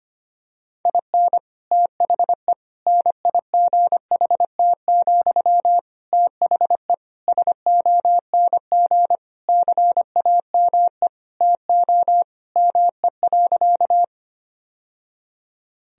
Morse code, heard as a keyed tone, about 25 wpm, 700 Hz. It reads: IN THE NIGHT, THE SONG CAME TO ME.
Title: Sentences from Top 1000 Words - Review Part 2 - 25wpm